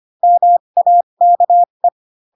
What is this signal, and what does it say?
Morse code, heard as a keyed tone, about 25 wpm, spaced as for 22 wpm, 700 Hz. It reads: MAKE
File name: Pinocchio / Pinocchio ch8 18 25 ik1hge